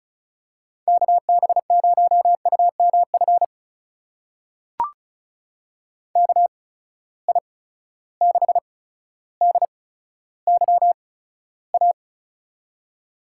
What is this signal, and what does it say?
Morse code, heard as a keyed tone, about 36 wpm, spaced as for 15 wpm, 700 Hz. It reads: KB0UMF KI6DYA
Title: USA Call Signs - 6 Characters - 8x Character Spacing - 35wpm